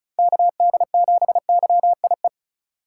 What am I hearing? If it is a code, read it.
Morse code, heard as a keyed tone, about 34 wpm, 700 Hz. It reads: KD7YIE
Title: USA Call Signs - 6 Characters - 2x Character Spacing - 35wpm